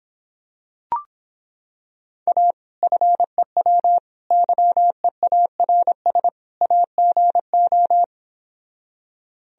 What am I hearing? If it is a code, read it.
Morse code, heard as a keyed tone, about 26 wpm, 700 Hz. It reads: A FEW YEARS AGO